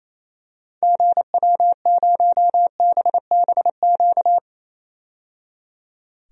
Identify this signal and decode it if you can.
Morse code, heard as a keyed tone, about 28 wpm, 700 Hz. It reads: GW0BBQ
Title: DXCC-100 Call Signs - 6 Characters - 28wpm